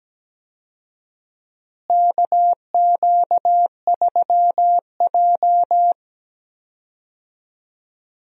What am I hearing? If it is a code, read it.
Morse code, heard as a keyed tone, about 17 wpm, 700 Hz. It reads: KQ3J
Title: USA Call Signs - 4 Characters - Repeated 3x - 17wpm